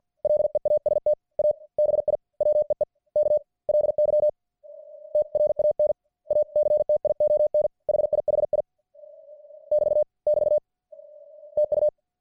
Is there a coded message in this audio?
Yes: Morse code from 0.2 to 11.9 s, heard as a keyed tone, about 50 wpm, spaced as for 40 wpm, 600 Hz. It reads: LEAST A BIWEEKLY TRANACTION HIHI <BT> <BT> TU